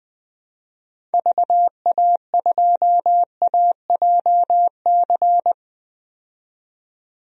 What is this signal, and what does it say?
Morse code, heard as a keyed tone, about 20 wpm, 700 Hz. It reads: VA2AJC
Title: DXCC-100 Call Signs - 6 Characters - 20wpm